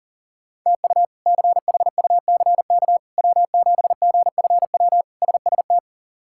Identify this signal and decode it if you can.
Morse code, heard as a keyed tone, about 40 wpm, 700 Hz. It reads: TU CHUCK W7GFW SST
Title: K1USN SST QSO Practice - 40wpm